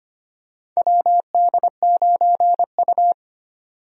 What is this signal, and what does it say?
Morse code, heard as a keyed tone, about 25 wpm, 700 Hz. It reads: WD9U
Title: USA Call Signs - 4 Characters - 25wpm